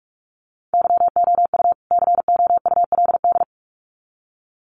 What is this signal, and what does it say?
Morse code, heard as a keyed tone, about 44 wpm, 700 Hz. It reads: YOU COULD
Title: Sets of 2 Words 45wpm